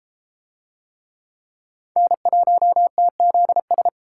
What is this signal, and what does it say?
Morse code, heard as a keyed tone, about 32 wpm, 700 Hz. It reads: N1TZS